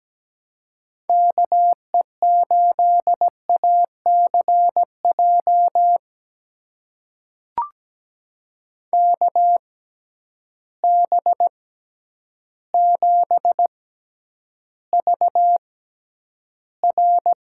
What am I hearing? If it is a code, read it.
Morse code, heard as a keyed tone, about 17 wpm, spaced as for 9 wpm, 700 Hz. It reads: KE8ACJ KB7VR
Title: USA Call Signs - 6 Characters - 6x Character Spacing - 17wpm